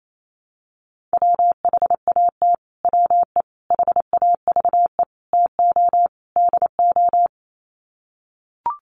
Morse code, heard as a keyed tone, about 28 wpm, 700 Hz. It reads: WHAT WE HAVE TO DO
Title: Sets of 5 Words Encore 28wpm